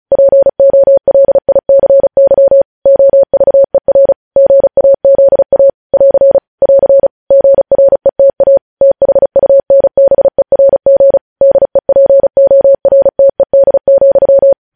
Morse code, heard as a keyed tone, about 36 wpm, 550 Hz. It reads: POLICY OVER GAZA <AR> <AR> GRETA THUNBERG DEPORTED,
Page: news headlines in morse code at 35 wpm